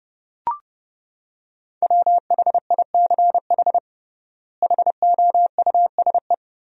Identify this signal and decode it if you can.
Morse code, heard as a keyed tone, about 32 wpm, 700 Hz. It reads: WHICH HOUSE
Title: 3x Word-Spacing - Sentences from Top 200 Words - 30wpm